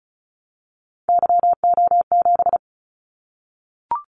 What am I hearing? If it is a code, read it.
Morse code, heard as a keyed tone, about 36 wpm, 700 Hz. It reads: YO7
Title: Callsign Prefixes for Top 25 DXCC - 35wpm